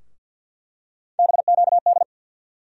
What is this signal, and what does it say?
Morse code, heard as a keyed tone, about 48 wpm, 700 Hz. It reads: BQD
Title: USA Call Sign Postfixes - 3 Letters 50wpm